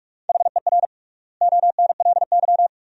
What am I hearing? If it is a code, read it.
Morse code, heard as a keyed tone, about 46 wpm, 700 Hz. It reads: HER ONLY
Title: 3x Word-Spacing - Sentences from Top 400 Words - 45wpm